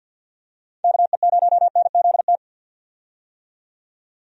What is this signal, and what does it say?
Morse code, heard as a keyed tone, about 51 wpm, 700 Hz. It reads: KE0NZT
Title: USA Call Signs - 6 Characters - 50wpm